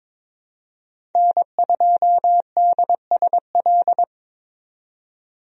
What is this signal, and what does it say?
Morse code, heard as a keyed tone, about 22 wpm, 700 Hz. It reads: N2DSL
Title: USA Call Signs - 5 Characters - 22wpm